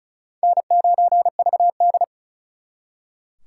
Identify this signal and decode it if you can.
Morse code, heard as a keyed tone, about 46 wpm, 700 Hz. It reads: N9VD